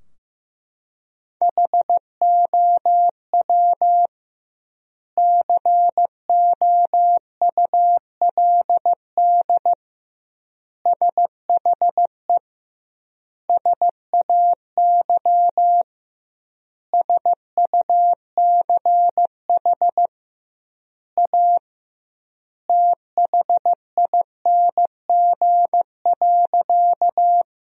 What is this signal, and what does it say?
Morse code, heard as a keyed tone, about 15 wpm, 700 Hz. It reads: HOW COULD SHE SAY SUCH A THING.
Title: 2x Word-Spacing - Sentences from Top 200 Words - 15wpm